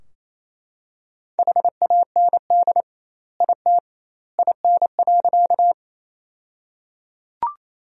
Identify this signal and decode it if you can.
Morse code, heard as a keyed tone, about 27 wpm, 700 Hz. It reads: HAND IT IN.
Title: 2x Word-Spacing - Sentences from Top 200 Words - 28wpm